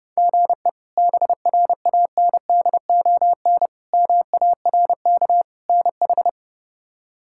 Morse code, heard as a keyed tone, about 30 wpm, 700 Hz. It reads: GE BRANDON MARK NH
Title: K1USN SST QSO Practice - 30wpm